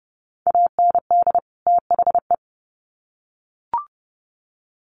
Morse code, heard as a keyed tone, about 30 wpm, 700 Hz. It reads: AND THE
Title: Sets of 2 Words 30wpm